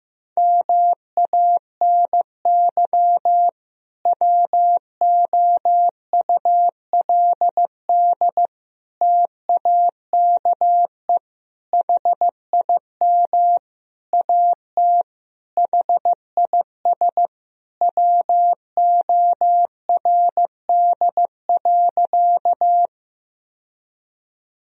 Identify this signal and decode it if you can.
Morse code, heard as a keyed tone, about 15 wpm, 700 Hz. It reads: MANY WOULD TAKE HIM AT HIS WORD.
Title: Sentences from Top 1000 Words - Review Part 1 - 15wpm